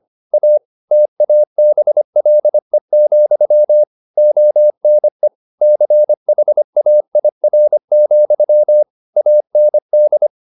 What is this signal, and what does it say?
Morse code, heard as a keyed tone, about 25 wpm, 600 Hz. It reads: A TABLE, ONE CHAIR, AND
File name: the old man and the sea part 1 25wpm